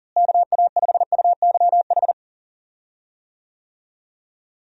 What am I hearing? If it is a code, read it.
Morse code, heard as a keyed tone, about 39 wpm, 700 Hz. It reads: KA5UYH
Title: USA Call Signs - 6 Characters - Repeated 3x - 40wpm